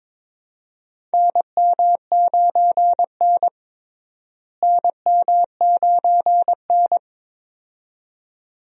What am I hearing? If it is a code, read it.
Morse code, heard as a keyed tone, about 22 wpm, 700 Hz. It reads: NM9N NM9N
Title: USA Call Signs - 4 Characters - Repeated 2x - 22wpm